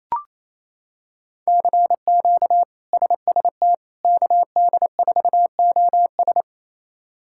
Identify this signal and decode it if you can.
Morse code, heard as a keyed tone, about 27 wpm, 700 Hz. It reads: CQ SST KD4OS
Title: K1USN SST QSO Practice - 28wpm